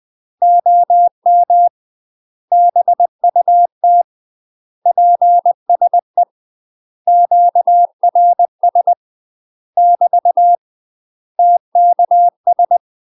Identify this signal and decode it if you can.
Morse code, heard as a keyed tone, about 20 wpm, 700 Hz. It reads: OM BUT PSE QRS <BT> TKS
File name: UK MASTER QSOs TRAINING / 20WPM / UK MASTER QSOs 0040